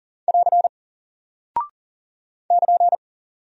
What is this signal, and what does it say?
Morse code, heard as a keyed tone, about 40 wpm, 700 Hz. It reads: <AR> <KN>
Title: Single Character (Letter:Number:Punctuation:Prosign) Alt 40wpm